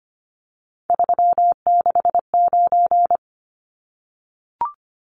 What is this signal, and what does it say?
Morse code, heard as a keyed tone, about 25 wpm, 700 Hz. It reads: 369
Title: Numbers - 3-Digit 25wpm